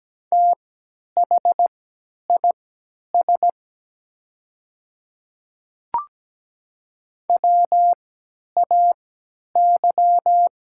Morse code, heard as a keyed tone, about 17 wpm, spaced as for 10 wpm, 700 Hz. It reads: THIS WAY
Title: 3x Character Spacing - Top 100 Words - 17wpm